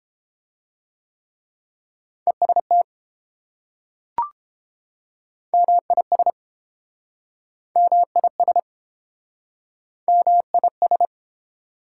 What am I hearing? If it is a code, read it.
Morse code, heard as a keyed tone, about 29 wpm, 700 Hz. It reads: EST MIS MIS MIS